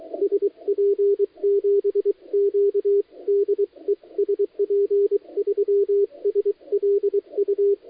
Morse code, heard as a keyed tone, about 23 wpm, 400 Hz, with radio noise. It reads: SP7Q DE SP3SLU
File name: dlazaawansowanych / nagraniazpasma / titawaNETcontrol